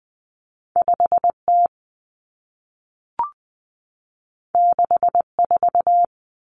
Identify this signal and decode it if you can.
Morse code, heard as a keyed tone, about 20 wpm, 700 Hz. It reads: 5T 64
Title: Common Cut Numbers - 2-Digit 20wpm